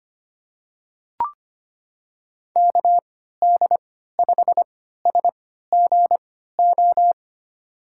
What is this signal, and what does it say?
Morse code, heard as a keyed tone, about 25 wpm, spaced as for 14 wpm, 700 Hz. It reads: KD5SGO
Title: USA Call Signs - 6 Characters - 3x Character Spacing - 25wpm